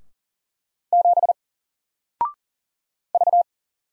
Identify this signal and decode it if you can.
Morse code, heard as a keyed tone, about 42 wpm, 700 Hz. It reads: 7 V
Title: Single Letter-Number Alt 40wpm